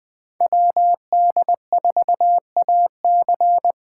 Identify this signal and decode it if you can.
Morse code, heard as a keyed tone, about 20 wpm, 700 Hz. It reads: WD4AC
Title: Speed Racing - USA Call Signs 17wpm